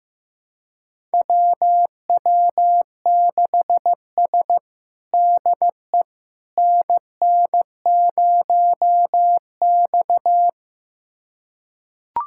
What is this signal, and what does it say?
Morse code, heard as a keyed tone, about 15 wpm, 700 Hz. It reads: WW6S DE NN0X
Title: USA Callsign DE Callsign - 4 Characters - 15wpm